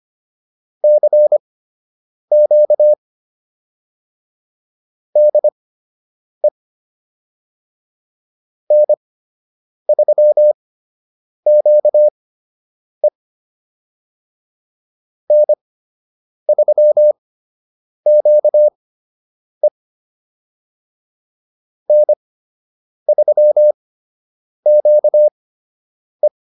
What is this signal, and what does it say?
Morse code, heard as a keyed tone, about 25 wpm, spaced as for 8 wpm, 600 Hz. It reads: CQ DE N3QE N3QE N3QE